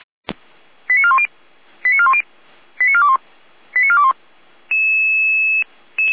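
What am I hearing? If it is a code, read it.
Five-tone selective call (ZVEI), heard as quick runs of five tones, tones of about 70 ms. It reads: 89410 89410 89421 89421
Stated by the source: Received by SDR.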